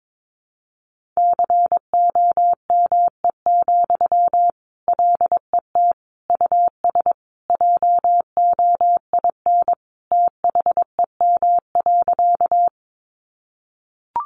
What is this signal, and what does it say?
Morse code, heard as a keyed tone, about 22 wpm, 700 Hz. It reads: COME, LET US JOIN THEM.
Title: Sentences from Top 1400 Words 22wpm